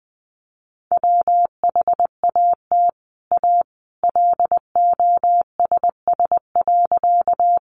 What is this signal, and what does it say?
Morse code, heard as a keyed tone, about 20 wpm, 700 Hz. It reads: WHAT A LOSS.
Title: Sentences from Top 1400 Words 20wpm